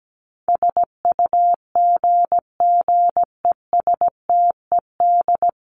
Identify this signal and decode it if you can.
Morse code, heard as a keyed tone, about 17 wpm, 700 Hz. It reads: SUGGESTED
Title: ED Suffix - Top 100 Words with Base Words 17wpm